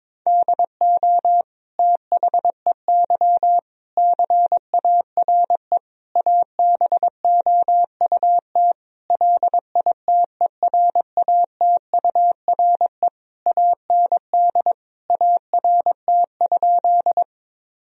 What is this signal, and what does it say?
Morse code, heard as a keyed tone, about 22 wpm, 700 Hz. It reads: DO THEY CARE ABOUT LITERATURE AND ART?
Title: Sentences from Top 1000 Words - Review Part 3 - 22wpm